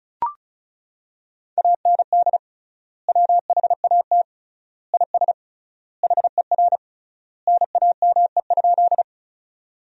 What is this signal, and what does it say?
Morse code, heard as a keyed tone, about 36 wpm, 700 Hz. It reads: AND WHAT IS HER NAME?
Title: 3x Word-Spacing - Sentences from Top 200 Words - 35wpm